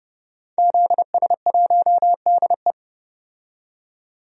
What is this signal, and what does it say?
Morse code, heard as a keyed tone, about 30 wpm, 700 Hz. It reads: ZS1DE
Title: DXCC-100 Call Signs - 5 Characters - 30wpm